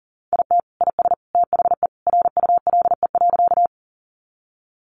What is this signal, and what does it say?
Morse code, heard as a keyed tone, about 40 wpm, 700 Hz. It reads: IT IS THE RULE.
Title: Sentences from Top 400 Words 40wpm